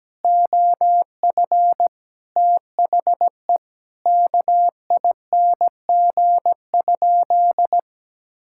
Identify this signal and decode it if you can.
Morse code, heard as a keyed tone, about 17 wpm, 700 Hz. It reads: OF THE KING?
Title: Sentences from Top 1000 Words - Review Part 6 - 17wpm